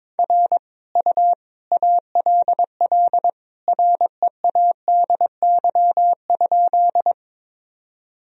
Morse code, heard as a keyed tone, about 22 wpm, 700 Hz. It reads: R U ALL READY?
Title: Sentences from Top 400 Words and Abbreviations 22wpm